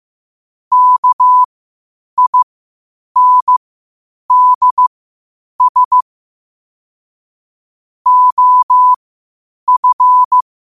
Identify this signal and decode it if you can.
Morse code, heard as a keyed tone, about 15 wpm, spaced as for 8 wpm, 1000 Hz. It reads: KINDS OF
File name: qotdfiles / 8wpm